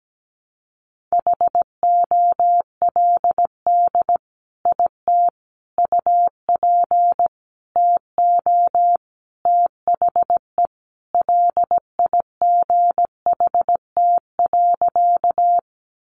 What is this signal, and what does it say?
Morse code, heard as a keyed tone, about 17 wpm, 700 Hz. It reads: HOLD IT UP TO THE LIGHT.
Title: Sentences from Top 400 Words 17wpm